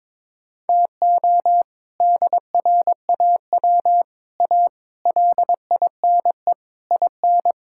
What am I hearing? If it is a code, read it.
Morse code, heard as a keyed tone, about 22 wpm, 700 Hz. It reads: TO DRAW A LINE IN
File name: practice / 22wpm / Sets of 7 Words - 22wpm